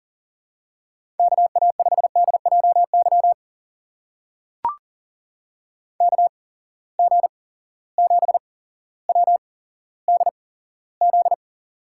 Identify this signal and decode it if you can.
Morse code, heard as a keyed tone, about 41 wpm, spaced as for 17 wpm, 700 Hz. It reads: KA5DJY KG7WDZ
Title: USA Call Signs - 6 Characters - 8x Character Spacing - 40wpm